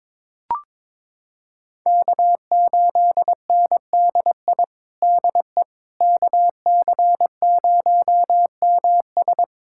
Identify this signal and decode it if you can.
Morse code, heard as a keyed tone, about 21 wpm, 700 Hz. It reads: K8NDI DE KC0MS
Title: USA Callsign DE Callsign - 5 Characters - 22wpm